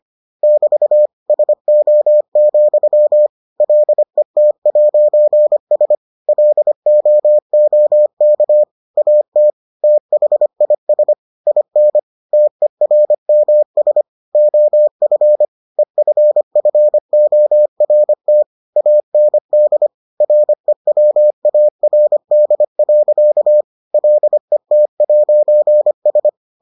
Morse code, heard as a keyed tone, about 25 wpm, 600 Hz. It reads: <BT> SO, LET'S LOOK AT THIS IN TERMS OF EFFORT AND REWARD. LET'S